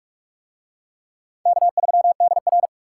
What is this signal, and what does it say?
Morse code, heard as a keyed tone, about 38 wpm, 700 Hz. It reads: K3DR